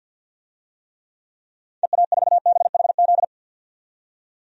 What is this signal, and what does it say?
Morse code, heard as a keyed tone, about 49 wpm, 700 Hz. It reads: EA4BHZ
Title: DXCC-100 Call Signs - 6 Characters - 50wpm